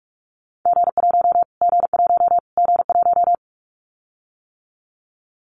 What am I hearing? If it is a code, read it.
Morse code, heard as a keyed tone, about 44 wpm, 700 Hz. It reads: G1 G1 G1
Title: Callsign Prefixes for Top 25 DXCC - Repeated 3x - 45wpm